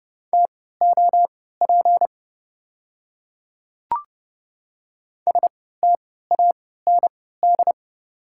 Morse code, heard as a keyed tone, about 32 wpm, spaced as for 17 wpm, 690 Hz. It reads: TOP STAND
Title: 3x Character Spacing - Top 500 Words - 30wpm